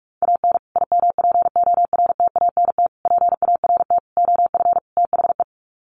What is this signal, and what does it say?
Morse code, heard as a keyed tone, about 44 wpm, 700 Hz. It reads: AN IMPORTANT PART OF THE
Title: Sets of 5 Words Encore 45wpm